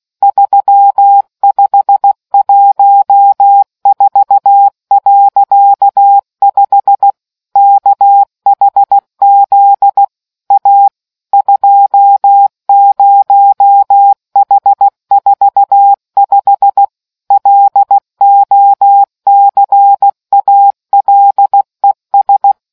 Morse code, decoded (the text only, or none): 3514.5 KHZ A 20H45 LOCALES